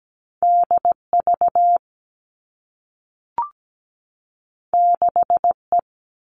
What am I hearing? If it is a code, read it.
Morse code, heard as a keyed tone, about 17 wpm, 700 Hz. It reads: DV 6E
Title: All Cut Numbers - 2-Digit 17wpm